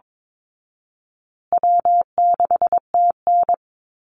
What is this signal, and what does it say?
Morse code, heard as a keyed tone, about 22 wpm, 700 Hz. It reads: W6TN
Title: Easy Call Signs Repeated 3x - 22wpm